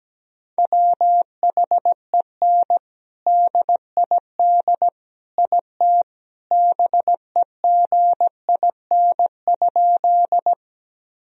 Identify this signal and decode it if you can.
Morse code, heard as a keyed tone, about 17 wpm, 700 Hz. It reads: WHEN DID IT BEGIN?
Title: Sentences from Top 1000 Words - Review Part 1 - 17wpm